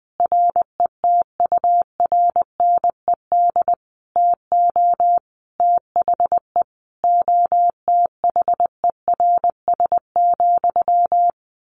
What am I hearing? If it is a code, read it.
Morse code, heard as a keyed tone, about 20 wpm, 700 Hz. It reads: RETURNED TO THE OTHERS,
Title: A Sound Of Thunder 20wpm